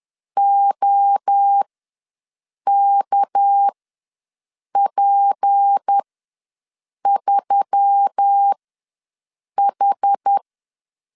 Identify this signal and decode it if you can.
Morse code, heard as a keyed tone, about 10 wpm, spaced as for 6 wpm, 800 Hz. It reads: OKP3H
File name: LEZIONI / 6 PKX0 / 6 PK ws 10wpm 3745